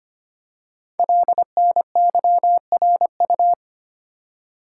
Morse code, heard as a keyed tone, about 25 wpm, 700 Hz. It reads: LNYRU
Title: Code Groups - 5 Letter:Numbers - 25wpm